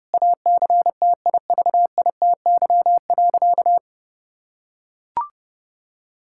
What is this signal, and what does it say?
Morse code, heard as a keyed tone, about 30 wpm, 700 Hz. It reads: ACTIVITY.
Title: Sentences from Top 1000 Words - Review Part 5 - 30wpm